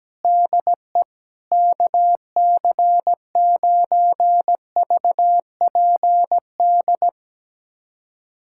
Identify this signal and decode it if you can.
Morse code, heard as a keyed tone, about 17 wpm, 700 Hz. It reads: DE KC9VPD
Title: USA Callsign DE Callsign - 6 Characters - 17wpm